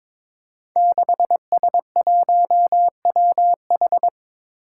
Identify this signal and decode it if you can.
Morse code, heard as a keyed tone, about 22 wpm, 700 Hz. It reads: 6S1WH
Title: Five Challenging Letter-Number Combinations 22wpm